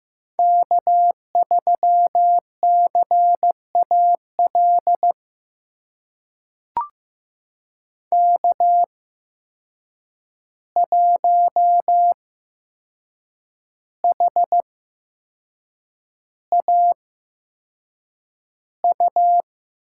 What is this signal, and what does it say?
Morse code, heard as a keyed tone, about 15 wpm, spaced as for 6 wpm, 700 Hz. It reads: K3CAL K1HAU
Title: USA Call Signs - 5 Characters - 8x Character Spacing - 15wpm